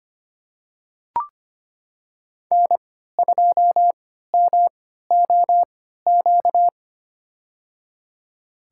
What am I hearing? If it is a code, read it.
Morse code, heard as a keyed tone, about 25 wpm, spaced as for 14 wpm, 700 Hz. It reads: N2MOQ
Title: USA Call Signs - 5 Characters - 3x Character Spacing - 25wpm